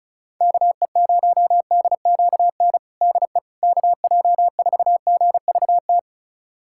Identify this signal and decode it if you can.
Morse code, heard as a keyed tone, about 34 wpm, 700 Hz. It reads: KE0DQN DE KJ4GVT
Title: USA Callsign DE Callsign - 6 Characters - 35wpm